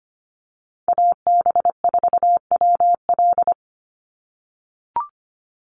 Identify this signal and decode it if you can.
Morse code, heard as a keyed tone, about 25 wpm, 700 Hz. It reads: AB4WL
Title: NA SOTA Chaser Callsigns 25wpm